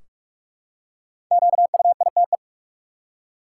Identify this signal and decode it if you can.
Morse code, heard as a keyed tone, about 46 wpm, 700 Hz. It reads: QUITE
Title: Five Letter Words - 45wpm